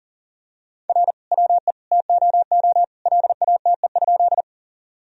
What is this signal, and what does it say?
Morse code, heard as a keyed tone, about 37 wpm, 700 Hz. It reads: R WE TOO LATE?
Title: Sentences from Top 300 Words and Abbreviations 40wpm